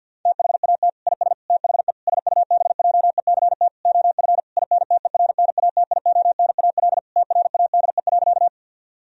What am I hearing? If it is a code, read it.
Morse code, heard as a keyed tone, about 50 wpm, 700 Hz. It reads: THAT IS THE SUBJECT OF INTERNATIONAL TRADE.
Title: Sentences from Top 1000 Words - Review Part 4 - 50wpm